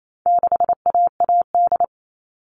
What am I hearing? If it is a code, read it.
Morse code, heard as a keyed tone, about 28 wpm, 700 Hz. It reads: 6AAD
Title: All Cut Numbers - 4-Digit 28wpm